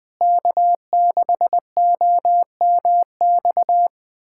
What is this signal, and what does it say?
Morse code, heard as a keyed tone, about 20 wpm, 700 Hz. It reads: K6OMX
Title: USA Callsign DE Callsign - 5 Characters - 20wpm